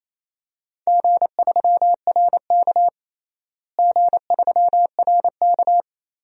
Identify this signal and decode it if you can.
Morse code, heard as a keyed tone, about 28 wpm, 700 Hz. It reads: G3RK G3RK
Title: DXCC-100 Call Signs - 4 Characters - Repeated 2x - 28wpm